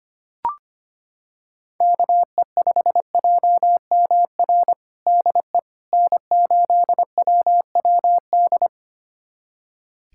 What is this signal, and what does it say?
Morse code, heard as a keyed tone, about 25 wpm, 700 Hz. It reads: KE5JMR DE N8WWD
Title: USA Callsign DE Callsign 25wpm